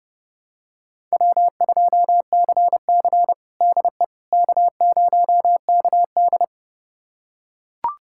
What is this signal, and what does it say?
Morse code, heard as a keyed tone, about 30 wpm, 700 Hz. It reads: W2CC DE K0KD
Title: USA Callsign DE Callsign - 4 Characters - 30wpm